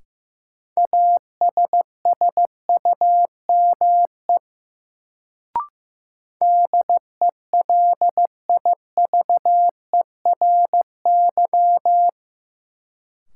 Morse code, heard as a keyed tone, about 16 wpm, 700 Hz. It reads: ASSUME DELIVERY
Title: Sentences from Top 1500 Words - New Word Drill - 15wpm